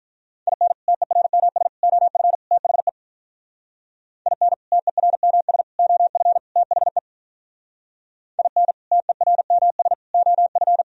Morse code, heard as a keyed tone, about 45 wpm, 700 Hz. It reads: IN TERMS OF THE IN TERMS OF THE IN TERMS OF